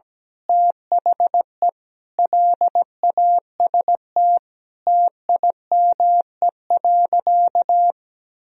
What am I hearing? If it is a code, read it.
Morse code, heard as a keyed tone, about 17 wpm, 700 Hz. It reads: THE LAST TIME.
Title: Sentences from Top 1000 Words - Review Part 2 - 17wpm